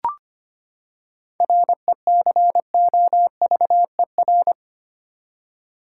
Morse code, heard as a keyed tone, about 25 wpm, 700 Hz. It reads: RECOVER
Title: Seven Letter Words - 25wpm